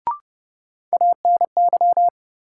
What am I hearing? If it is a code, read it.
Morse code, heard as a keyed tone, about 30 wpm, 700 Hz. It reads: ANY
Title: Sets of 3 Words - New Word Drill - 30wpm